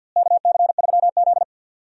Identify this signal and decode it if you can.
Morse code, heard as a keyed tone, about 52 wpm, 700 Hz. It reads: KC3Z
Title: USA Call Signs - 4 Characters - Repeated 3x - 50wpm